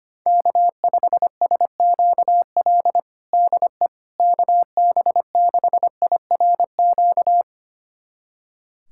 Morse code, heard as a keyed tone, about 25 wpm, 700 Hz. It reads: K5SQL DE KB6IRQ
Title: USA Callsign DE Callsign 25wpm